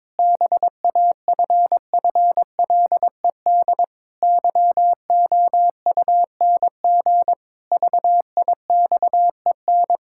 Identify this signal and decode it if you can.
Morse code, heard as a keyed tone, about 22 wpm, 700 Hz. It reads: BAFFLED YOUNG VIXEN
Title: Pangram Challenge v2 - 22wpm